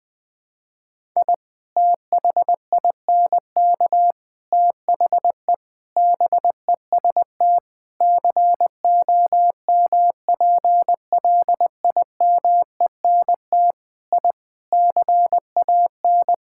Morse code, heard as a keyed tone, about 20 wpm, 700 Hz. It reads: I THINK THE BEST COMPLIMENT I CAN
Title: Sets of 8 Words - 20wpm